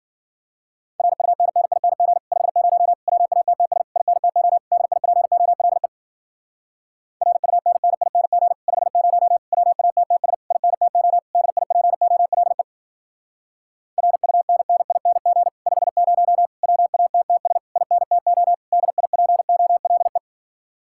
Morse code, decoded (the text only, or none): RUNNING 50 WATTS INTO DIPOLE RUNNING 50 WATTS INTO DIPOLE RUNNING 50 WATTS INTO DIPOLE